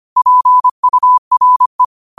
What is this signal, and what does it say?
Morse code, heard as a keyed tone, about 25 wpm, 1000 Hz. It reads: PURE